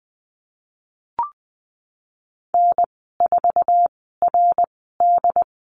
Morse code, heard as a keyed tone, about 20 wpm, spaced as for 15 wpm, 700 Hz. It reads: N4RD
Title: 076 Lesson 13g - Letter L - Call Signs 20wpm